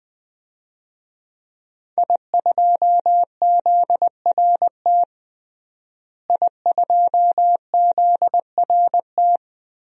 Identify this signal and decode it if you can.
Morse code, heard as a keyed tone, about 20 wpm, 700 Hz. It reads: I2ZRT I2ZRT
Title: DXCC-100 Call Signs - 5 Characters - Repeated 3x - 20wpm